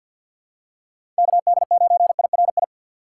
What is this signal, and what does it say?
Morse code, heard as a keyed tone, about 48 wpm, 700 Hz. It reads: KD9IRI
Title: USA Call Signs - 6 Characters - 4x Character Spacing - 50wpm